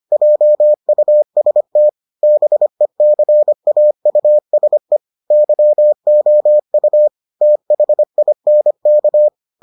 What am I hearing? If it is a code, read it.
Morse code, heard as a keyed tone, about 25 wpm, 600 Hz. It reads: JUST BECAUSE YOU THINK